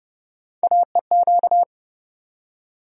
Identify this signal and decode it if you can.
Morse code, heard as a keyed tone, about 32 wpm, 700 Hz. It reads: AEQ
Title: USA Call Sign Postfixes - 3 Letters 30wpm